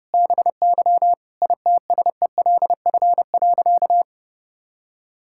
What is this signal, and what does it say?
Morse code, heard as a keyed tone, about 32 wpm, 700 Hz. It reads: BY ITSELF.
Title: Sentences from Top 1000 Words - Review Part 5 - 30wpm